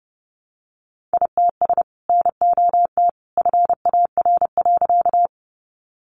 Morse code, heard as a keyed tone, about 29 wpm, 700 Hz. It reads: ITS NOT FAR.
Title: Sentences from Top 600 Words 30wpm